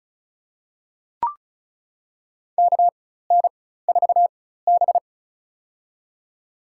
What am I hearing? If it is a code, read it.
Morse code, heard as a keyed tone, about 34 wpm, spaced as for 16 wpm, 700 Hz. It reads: KN4B
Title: USA Call Signs - 4 Characters - 4x Character Spacing - 35wpm